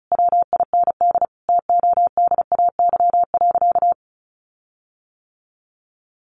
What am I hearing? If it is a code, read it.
Morse code, heard as a keyed tone, about 34 wpm, 700 Hz. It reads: WIND TODAY.
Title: Sentences from Top 1400 Words 35wpm